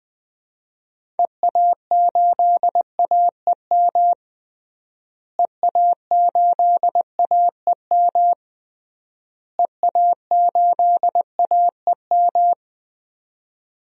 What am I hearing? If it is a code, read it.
Morse code, heard as a keyed tone, about 20 wpm, 700 Hz. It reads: EA8AEM EA8AEM EA8AEM
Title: DXCC-100 Call Signs - 6 Characters - Repeated 3x - 20wpm